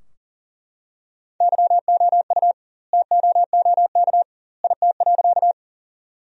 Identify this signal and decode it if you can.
Morse code, heard as a keyed tone, about 40 wpm, 700 Hz. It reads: YOU TOOK IT.
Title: 2x Word-Spacing - Sentences from Top 300 Words - 40wpm